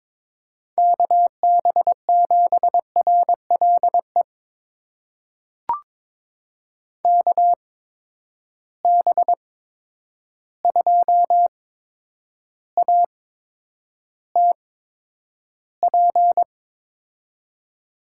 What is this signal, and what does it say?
Morse code, heard as a keyed tone, about 22 wpm, spaced as for 9 wpm, 700 Hz. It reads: KB7RLE KB2ATP
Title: USA Call Signs - 6 Characters - 8x Character Spacing - 22wpm